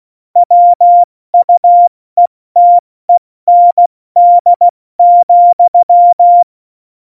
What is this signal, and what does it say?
Morse code, heard as a keyed tone, about 16 wpm, spaced as for 14 wpm, 700 Hz. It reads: WUETEND,
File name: Morsekurs / Witze / Witz 15. Oktober 2019, 70 F80